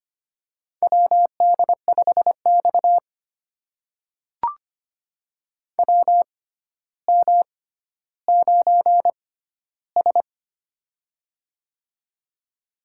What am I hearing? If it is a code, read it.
Morse code, heard as a keyed tone, about 25 wpm, spaced as for 13 wpm, 700 Hz. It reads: WD5X WM9S
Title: USA Call Signs - 4 Characters - 6x Character Spacing - 25wpm